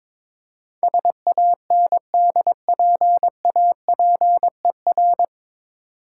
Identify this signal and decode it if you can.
Morse code, heard as a keyed tone, about 22 wpm, 690 Hz. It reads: SANDPAPER
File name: Difficult Words with Hints - 22wpm